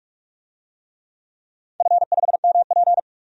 Morse code, heard as a keyed tone, about 46 wpm, 700 Hz. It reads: F5MP
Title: DXCC-100 Call Signs - 4 Characters - Repeated 2x - 45wpm